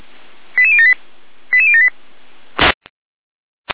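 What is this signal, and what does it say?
Five-tone selective call (ZVEI), heard as quick runs of five tones, tones of about 80 ms. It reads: 88078 88078
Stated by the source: ZVEI five-tone selective call received on an SDR.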